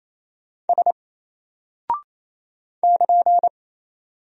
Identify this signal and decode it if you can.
Morse code, heard as a keyed tone, about 27 wpm, 700 Hz. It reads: S <KN>
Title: Single Character (Letter:Number:Punctuation:Prosign) 28wpm